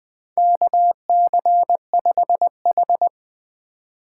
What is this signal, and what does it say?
Morse code, heard as a keyed tone, about 21 wpm, 700 Hz. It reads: KC5H
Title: 2023 POTA Hunter Call Signs - 20wpm